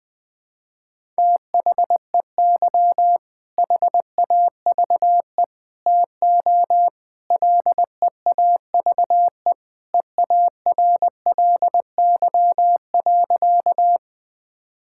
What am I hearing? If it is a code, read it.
Morse code, heard as a keyed tone, about 21 wpm, 700 Hz. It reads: THEY HAVE TO LEAVE EARLY.
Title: Sentences with 3-Word Phrases - 20wpm